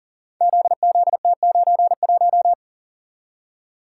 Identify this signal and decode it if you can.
Morse code, heard as a keyed tone, about 39 wpm, 700 Hz. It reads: ZZT91
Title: Code Groups - 5 Letter:Numbers - 40wpm